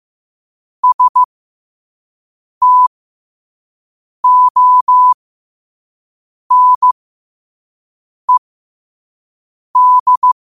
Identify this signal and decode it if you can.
Morse code, heard as a keyed tone, about 15 wpm, spaced as for 5 wpm, 1000 Hz. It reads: STONED
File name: qotdfiles / 5wpm